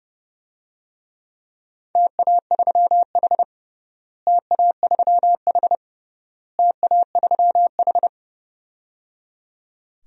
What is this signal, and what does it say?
Morse code, heard as a keyed tone, about 30 wpm, 700 Hz. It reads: TA3H TA3H TA3H
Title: DXCC-100 Call Signs - 4 Characters - Repeated 3x - 30wpm